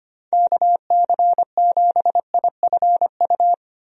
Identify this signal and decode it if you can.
Morse code, heard as a keyed tone, about 25 wpm, 700 Hz. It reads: KC7IFU
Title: USA Call Signs - 6 Characters - 25wpm